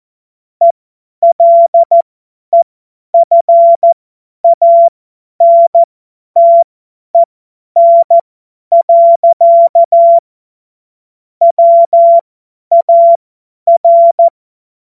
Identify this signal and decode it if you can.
Morse code, heard as a keyed tone, about 14 wpm, spaced as for 10 wpm, 680 Hz. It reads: ELEFANTEN. WAR
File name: Morsekurs / Witze / Witz 06. Januar 2026 50 F70